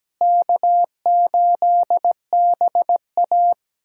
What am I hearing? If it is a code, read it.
Morse code, heard as a keyed tone, about 17 wpm, 700 Hz. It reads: K8BA
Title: USA Call Signs - 4 Characters - Repeated 3x - 17wpm